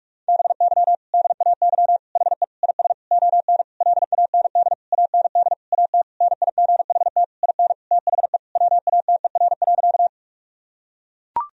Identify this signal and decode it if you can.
Morse code, heard as a keyed tone, about 42 wpm, 700 Hz. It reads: BY DAY HE IS ON LAND AND AT NIGHT IN THE WATER.
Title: Sentences from Top 1000 Words - Review Part 1 - 45wpm